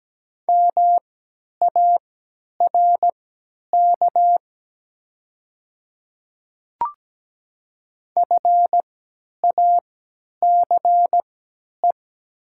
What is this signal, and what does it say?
Morse code, heard as a keyed tone, about 17 wpm, spaced as for 10 wpm, 700 Hz. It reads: MARK FACE
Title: 3x Character Spacing - Top 500 Words - 17wpm